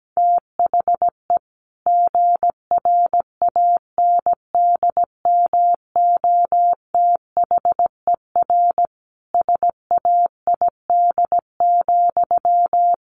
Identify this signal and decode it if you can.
Morse code, heard as a keyed tone, about 17 wpm, 700 Hz. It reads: THE GRANDMOTHER SAID,
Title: Sentences from Top 1200 Words 17wpm